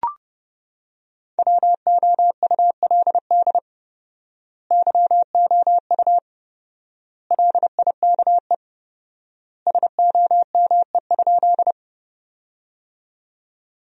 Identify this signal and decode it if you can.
Morse code, heard as a keyed tone, about 30 wpm, 700 Hz. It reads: WOULD YOU LIKE SOME?
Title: 4x Word-Spacing - Sentences from Top 100 Words - 30wpm